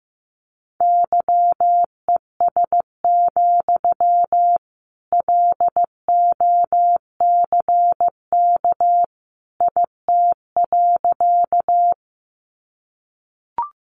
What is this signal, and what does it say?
Morse code, heard as a keyed tone, about 15 wpm, 700 Hz. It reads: YES, LOCK IT.
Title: Sentences from Top 1300 Words 15wpm